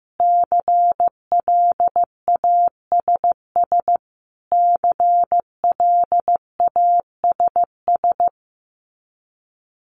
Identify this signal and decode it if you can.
Morse code, heard as a keyed tone, about 15 wpm, 700 Hz. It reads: CLASS CLASS
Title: Top 500 Words Repeated 15wpm